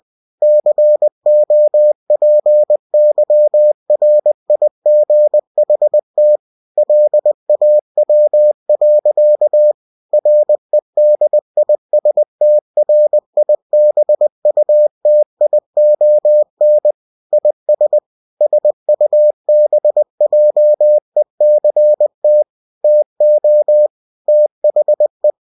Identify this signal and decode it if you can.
Morse code, heard as a keyed tone, about 20 wpm, 600 Hz. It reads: COPYRIGHT LAW. REDISTRIBUTION IS SUBJECT TO THE
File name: cw-books / winnie-the-pooh / winnie-20wpm-page-0112